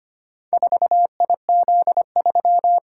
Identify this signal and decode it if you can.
Morse code, heard as a keyed tone, about 25 wpm, 700 Hz. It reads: 4IZ3
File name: practice / 25wpm / Four Letter-Number Combinations - 25wpm